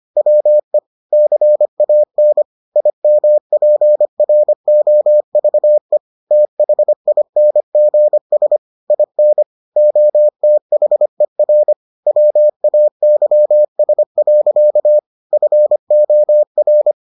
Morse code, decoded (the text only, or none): WE CAN IMPROVE THINGS IN OTHER WAYS. FOR